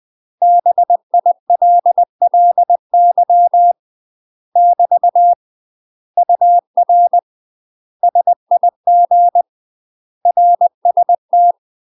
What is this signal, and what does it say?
Morse code, heard as a keyed tone, about 20 wpm, 700 Hz. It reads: BILLY <BT> UR SIG RST
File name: UK MASTER QSOs TRAINING / 20WPM / UK MASTER QSOs 0046